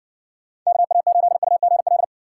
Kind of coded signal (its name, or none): Morse code